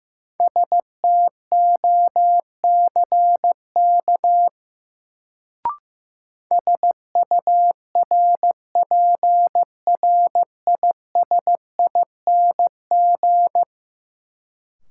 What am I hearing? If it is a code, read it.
Morse code, heard as a keyed tone, about 15 wpm, 700 Hz. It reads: STOCK SURPRISING
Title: Sets of 5 Words - New Word Drill - 15wpm